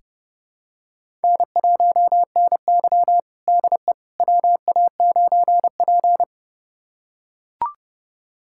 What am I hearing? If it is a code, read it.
Morse code, heard as a keyed tone, about 30 wpm, 700 Hz. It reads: N1NY DE WA9P
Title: USA Callsign DE Callsign - 4 Characters - 30wpm